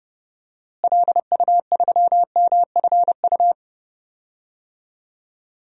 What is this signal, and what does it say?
Morse code, heard as a keyed tone, about 30 wpm, 700 Hz. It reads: LU3MFU
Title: DXCC-100 Call Signs - 6 Characters - Repeated 2x - 30wpm